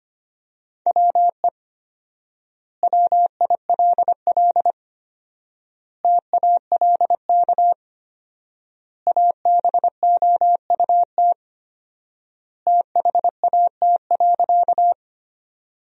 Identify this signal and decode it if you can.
Morse code, heard as a keyed tone, about 25 wpm, 700 Hz. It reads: WE WILL TALK ABOUT THAT.
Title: 4x Word-Spacing - Sentences from Top 400 Words - 25wpm